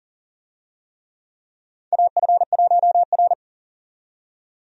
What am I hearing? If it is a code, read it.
Morse code, heard as a keyed tone, about 40 wpm, 700 Hz. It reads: AF1R
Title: USA Call Signs - 4 Characters - 40wpm